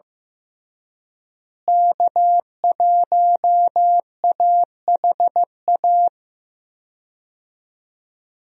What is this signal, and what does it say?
Morse code, heard as a keyed tone, about 15 wpm, 700 Hz. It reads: K1AHA
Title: USA Call Signs - 5 Characters - Repeated 3x - 15wpm